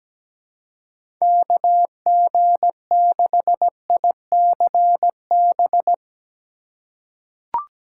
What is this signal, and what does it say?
Morse code, heard as a keyed tone, about 17 wpm, 700 Hz. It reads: KG6ICB
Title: USA Call Signs - 3x Character Spacing - 17wpm